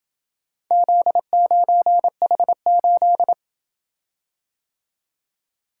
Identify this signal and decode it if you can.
Morse code, heard as a keyed tone, about 28 wpm, 700 Hz. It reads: Z9H8